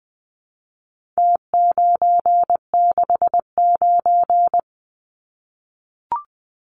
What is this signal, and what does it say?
Morse code, heard as a keyed tone, about 20 wpm, 700 Hz. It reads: T969
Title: Numbers - Contest Style - 4-Digit 20wpm